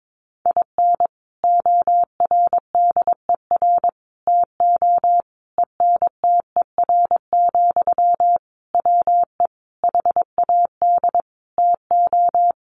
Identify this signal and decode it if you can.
Morse code, heard as a keyed tone, about 22 wpm, 700 Hz. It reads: IN ORDER TO ENTER, WE HAD TO
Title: Sentences from Top 1300 Words 22wpm